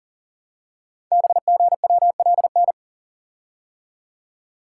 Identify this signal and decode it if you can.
Morse code, heard as a keyed tone, about 38 wpm, 690 Hz. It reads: BGWLN
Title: Code Groups - 5 Letters - 40wpm